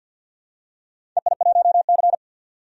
Morse code, heard as a keyed tone, about 47 wpm, 700 Hz. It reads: EI1C
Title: DXCC-100 Call Signs - 4 Characters - 50wpm